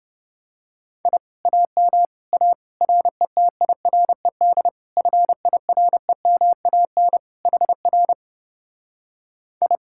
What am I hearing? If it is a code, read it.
Morse code, heard as a keyed tone, about 29 wpm, 700 Hz. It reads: I AM A RETIRED FIREMAN HR I